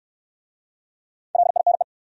Morse code, heard as a keyed tone, about 69 wpm, 700 Hz. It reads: LINE